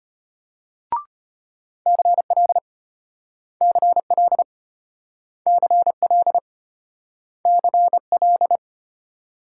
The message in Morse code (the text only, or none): CL CL CL CL